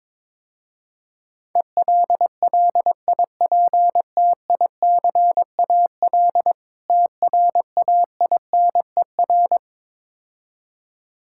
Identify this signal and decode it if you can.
Morse code, heard as a keyed tone, about 22 wpm, 700 Hz. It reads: ELLIPTICAL TRAINER
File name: Easy Words - Speed-Racing with Hints - 22wpm